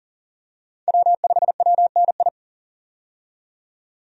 Morse code, heard as a keyed tone, about 37 wpm, 700 Hz. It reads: W5WNI
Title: USA Call Signs - 5 Characters - 40wpm